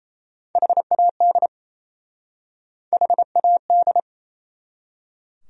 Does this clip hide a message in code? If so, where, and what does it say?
Yes: Morse code from 0.6 to 4.0 s, heard as a keyed tone, about 39 wpm, 700 Hz. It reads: HAD HAD